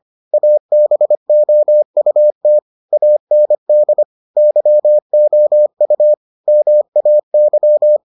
Morse code, heard as a keyed tone, about 25 wpm, 600 Hz. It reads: ABOUT AND YOU MAY